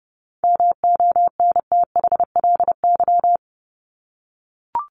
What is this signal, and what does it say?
Morse code, heard as a keyed tone, about 30 wpm, 700 Hz. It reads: MONTHLY
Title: LY Suffix - Top 100 Words with Base Words 30wpm